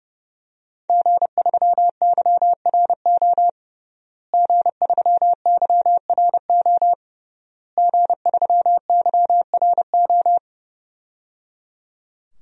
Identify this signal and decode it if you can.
Morse code, heard as a keyed tone, about 30 wpm, 700 Hz. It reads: G3YRO G3YRO G3YRO
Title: DXCC-100 Call Signs - 5 Characters - Repeated 3x - 30wpm